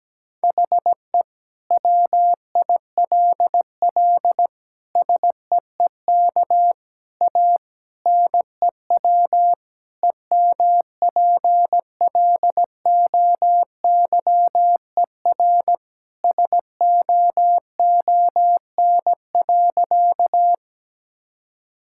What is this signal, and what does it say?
Morse code, heard as a keyed tone, about 17 wpm, 700 Hz. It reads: HE WILL SEEK A NEW EMPLOYER SOON.
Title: Sentences from Top 1600 Words 17wpm